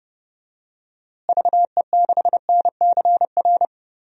Morse code, heard as a keyed tone, about 31 wpm, 700 Hz. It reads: VE6NCR
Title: DXCC-100 Call Signs - 6 Characters - Repeated 3x - 30wpm